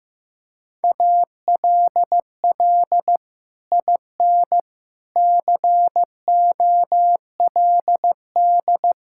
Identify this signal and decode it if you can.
Morse code, heard as a keyed tone, about 15 wpm, 700 Hz. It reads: ALL IN COLD
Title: Sentences from Top 1000 Words - Review Part 5 - 15wpm